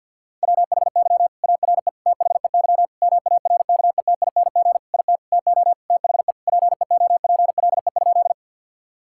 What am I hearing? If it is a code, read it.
Morse code, heard as a keyed tone, about 46 wpm, 690 Hz. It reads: WHY ARE THEY MARKETING IT TO THE PEOPLE?
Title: Sentences from Top 1000 Words - Review Part 5 - 50wpm